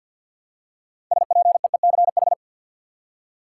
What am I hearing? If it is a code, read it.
Morse code, heard as a keyed tone, about 52 wpm, 700 Hz. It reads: SPEECH
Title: Six Letter Words - 50wpm